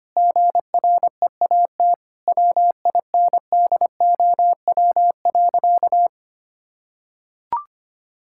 Morse code, heard as a keyed tone, about 25 wpm, 700 Hz. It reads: GREAT WINDOW.
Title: Sentences from Top 1600 Words 25wpm